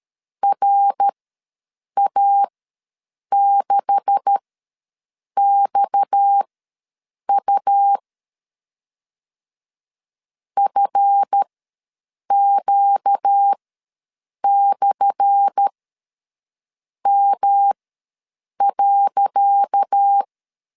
Morse code, heard as a keyed tone, about 13 wpm, spaced as for 7 wpm, 800 Hz. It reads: RA6XU FQ/M.
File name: LEZIONI / 10 SIGNS / 10C DOT ws 12wpm 3548